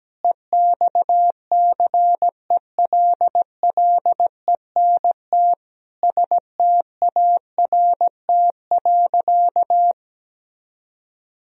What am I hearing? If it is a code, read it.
Morse code, heard as a keyed tone, about 17 wpm, 700 Hz. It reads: EXCELLENT START.
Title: Sentences from Top 1000 Words - Review Part 1 - 17wpm